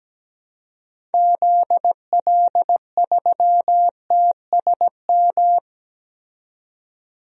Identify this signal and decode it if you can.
Morse code, heard as a keyed tone, about 17 wpm, 700 Hz. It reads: ZL3TSM
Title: DXCC-100 Call Signs - 6 Characters - 17wpm